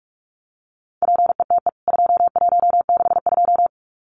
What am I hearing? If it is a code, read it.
Morse code, heard as a keyed tone, about 44 wpm, 700 Hz. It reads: PETE 2162
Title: CWT Exchange 45wpm